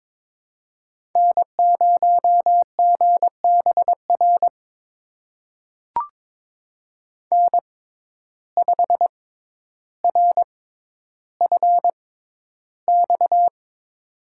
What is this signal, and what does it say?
Morse code, heard as a keyed tone, about 22 wpm, spaced as for 11 wpm, 700 Hz. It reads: N0GBR N5RFX
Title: USA Call Signs - 5 Characters - 6x Character Spacing - 22wpm